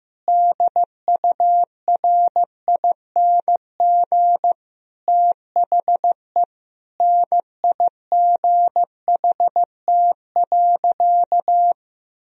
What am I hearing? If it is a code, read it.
Morse code, heard as a keyed tone, about 15 wpm, 700 Hz. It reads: DURING THE NIGHT.
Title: Sentences from Top 1000 Words - Review Part 4 - 15wpm